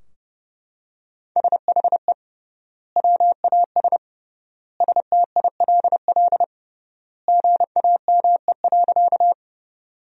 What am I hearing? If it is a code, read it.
Morse code, heard as a keyed tone, about 30 wpm, 700 Hz. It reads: SHE WAS STILL GAME.
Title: 3x Word-Spacing - Sentences from Top 500 Words - 30wpm